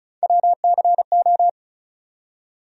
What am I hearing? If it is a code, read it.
Morse code, heard as a keyed tone, about 36 wpm, 700 Hz. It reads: WCO
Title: USA Call Sign Postfixes - 3 Letters 35wpm